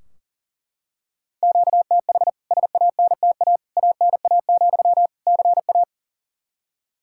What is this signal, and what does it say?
Morse code, heard as a keyed tone, about 40 wpm, 700 Hz. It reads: QTH SANTA ANA, CA
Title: K1USN SST QSO Practice 40wpm